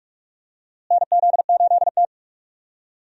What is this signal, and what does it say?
Morse code, heard as a keyed tone, about 42 wpm, 700 Hz. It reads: NZ8T
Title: USA Call Signs - 4 Characters - Repeated 3x - 45wpm